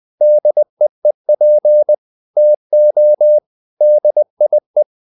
Morse code, heard as a keyed tone, about 20 wpm, 600 Hz. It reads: DEEP TO DIE
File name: the old man and the sea part 2 20wpm